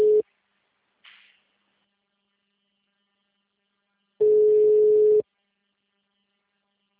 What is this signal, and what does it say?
Ringback tone (Europe, 425 Hz).